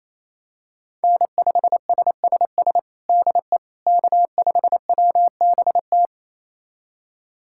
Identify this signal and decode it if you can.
Morse code, heard as a keyed tone, about 28 wpm, 700 Hz. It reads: N5SSS DE K5WBT
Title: USA Callsign DE Callsign - 5 Characters - 28wpm